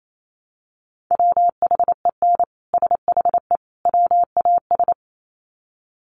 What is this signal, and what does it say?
Morse code, heard as a keyed tone, about 28 wpm, 700 Hz. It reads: WHEN SHE WAS
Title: Sets of 3 Words 28wpm